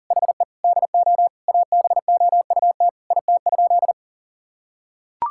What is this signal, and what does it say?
Morse code, heard as a keyed tone, about 40 wpm, 700 Hz. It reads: HE DO ABOUT IT?
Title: Sentences from Top 1000 Words - Review Part 6 - 40wpm